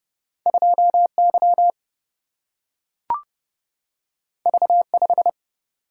Morse code, heard as a keyed tone, about 30 wpm, 700 Hz. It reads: 2Y V5
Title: Single-Letter Single-Number 30wpm